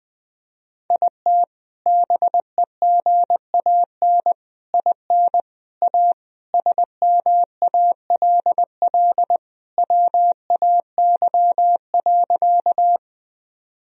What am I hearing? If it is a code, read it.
Morse code, heard as a keyed tone, about 20 wpm, 700 Hz. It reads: IT BEGAN IN A SMALL WAY.
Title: Sentences from Top 1000 Words - Review Part 1 - 20wpm